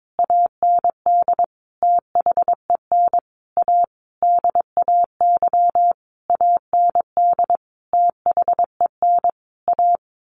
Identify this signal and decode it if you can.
Morse code, heard as a keyed tone, about 22 wpm, 700 Hz. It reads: AND THEN A DAY AND THEN A
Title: A Sound Of Thunder 22wpm